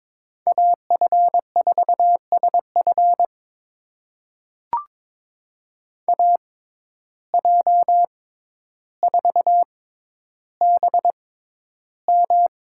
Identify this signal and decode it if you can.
Morse code, heard as a keyed tone, about 22 wpm, spaced as for 11 wpm, 700 Hz. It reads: AF4SF AJ4BM
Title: USA Call Signs - 5 Characters - 6x Character Spacing - 22wpm